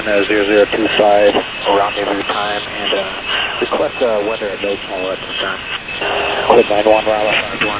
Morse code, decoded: none